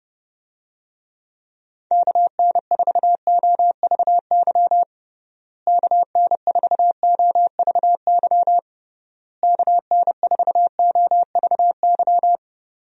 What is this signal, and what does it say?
Morse code, heard as a keyed tone, about 31 wpm, 700 Hz. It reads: KN4OVY KN4OVY KN4OVY
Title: USA Call Signs - 6 Characters - Repeated 3x - 30wpm